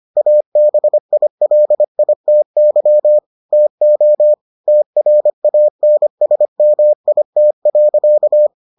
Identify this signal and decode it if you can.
Morse code, heard as a keyed tone, about 25 wpm, 600 Hz. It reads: ABILITY TO TRANSMIT.